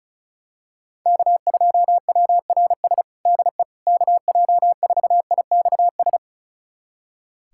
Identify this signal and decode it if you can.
Morse code, heard as a keyed tone, about 36 wpm, 700 Hz. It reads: K2WRS DE KJ4IXS
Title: USA Callsign DE Callsign 35wpm